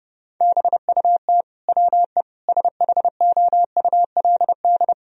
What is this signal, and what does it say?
Morse code, heard as a keyed tone, about 30 wpm, 700 Hz. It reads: BUT WE SHOULD
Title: Sentences from Top 1000 Words - Review Part 4 - 30wpm